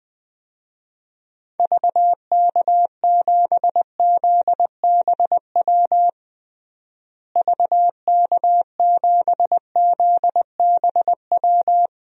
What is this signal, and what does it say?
Morse code, heard as a keyed tone, about 20 wpm, 700 Hz. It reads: VK7ZBW VK7ZBW
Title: DXCC-100 Call Signs - 6 Characters - Repeated 2x - 20wpm